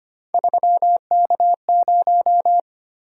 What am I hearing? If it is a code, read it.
Morse code, heard as a keyed tone, about 25 wpm, 700 Hz. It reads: 3K0
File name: Three Letter-Number Combinations - 25wpm